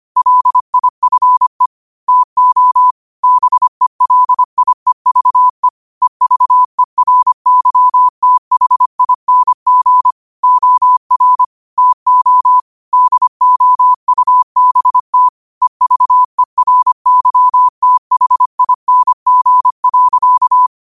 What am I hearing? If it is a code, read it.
Morse code, heard as a keyed tone, about 25 wpm, 1000 Hz. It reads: LIFE TO BELIEVE EVERYTHING OR TO DOUBT EVERYTHING.
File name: qotdfiles / 25wpm